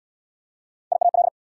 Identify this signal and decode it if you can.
Morse code, heard as a keyed tone, about 69 wpm, 700 Hz. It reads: IAL